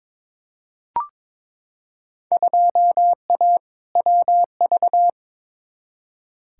Morse code, heard as a keyed tone, about 22 wpm, 700 Hz. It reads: 2A WV
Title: Field Day Exchanges Encore - 22wpm